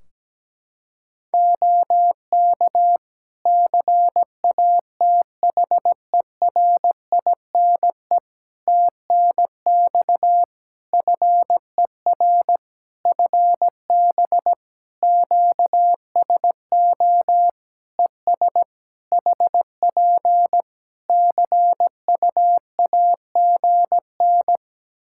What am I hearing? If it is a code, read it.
Morse code, heard as a keyed tone, about 17 wpm, 700 Hz. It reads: OK CATHERINE TNX FER FB QSO ES HP CUAGN
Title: Practice QSO - Exchange 4 - Ending - 17wpm